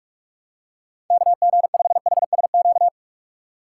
Morse code, heard as a keyed tone, about 44 wpm, 700 Hz. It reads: KG5HSQ
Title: USA Call Signs - 6 Characters - 6x Character Spacing - 45wpm